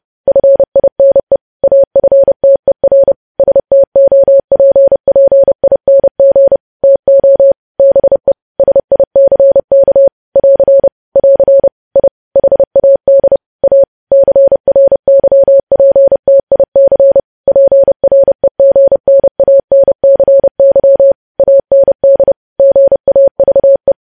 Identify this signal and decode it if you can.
Morse code, heard as a keyed tone, about 30 wpm, 550 Hz. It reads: FINE AFTER STOPPING TO BE SICK <AR> <AR> I HAD A CRYPTIC PREGNANCY AND GAVE
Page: news headlines in morse code at 30 wpm